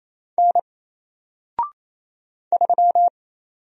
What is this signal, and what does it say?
Morse code, heard as a keyed tone, about 27 wpm, 700 Hz. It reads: N 3
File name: Single-Letter-Number-28wpm